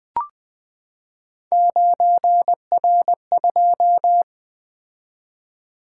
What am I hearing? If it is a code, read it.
Morse code, heard as a keyed tone, about 20 wpm, 700 Hz. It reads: 9R2
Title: Three Letter-Number Combinations - 20wpm